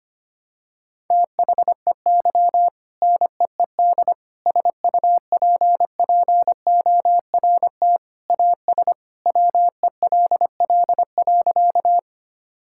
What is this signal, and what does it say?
Morse code, heard as a keyed tone, about 25 wpm, 700 Hz. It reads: THEY NEED SUPPORT AS WELL.
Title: Sentences from Top 1000 Words - Review Part 5 - 25wpm